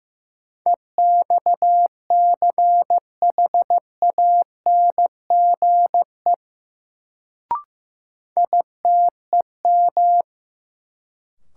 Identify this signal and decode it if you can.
Morse code, heard as a keyed tone, about 15 wpm, 700 Hz. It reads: EXCHANGE ITEM
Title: Sentences from Top 1000 Words - Review Part 6 - New Word Drill - 15wpm